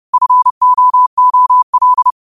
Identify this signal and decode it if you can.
Morse code, heard as a keyed tone, about 30 wpm, 1000 Hz. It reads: FOOL
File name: qotdfiles / 30wpm